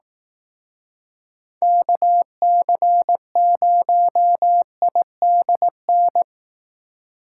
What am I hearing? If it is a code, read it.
Morse code, heard as a keyed tone, about 18 wpm, 700 Hz. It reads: KC0IDN
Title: Speed Racing - USA 2x3 Call Signs 15wpm